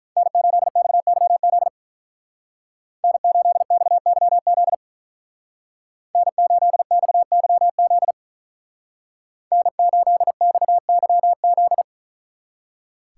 Morse code, heard as a keyed tone, about 43 wpm, 700 Hz. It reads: N8XYZ N8XYZ N8XYZ N8XYZ
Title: Speed Racing - USA 1x3 Call Signs 35wpm